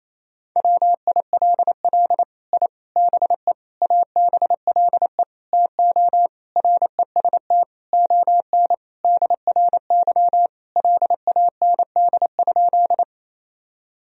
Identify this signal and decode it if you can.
Morse code, heard as a keyed tone, about 28 wpm, 700 Hz. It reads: WILL I BE ABLE TO REST ON DRY LAND?
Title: Sentences from Top 1000 Words - Review Part 3 - 28wpm